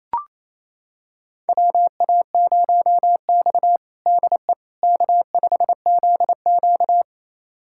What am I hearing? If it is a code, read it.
Morse code, heard as a keyed tone, about 28 wpm, 700 Hz. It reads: WA0X DE K5ZQ
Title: USA Callsign DE Callsign - 4 Characters - 28wpm